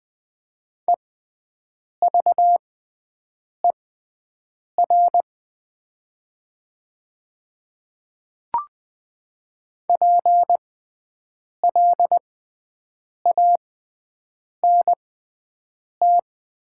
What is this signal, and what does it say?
Morse code, heard as a keyed tone, about 20 wpm, spaced as for 7 wpm, 700 Hz. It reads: EVER PLANT
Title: 6x Character Spacing - Top 500 Words - 20wpm